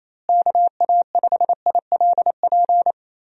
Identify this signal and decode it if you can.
Morse code, heard as a keyed tone, about 28 wpm, 700 Hz. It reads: KA5ILP
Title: USA Call Signs - 6 Characters - Repeated 2x - 28wpm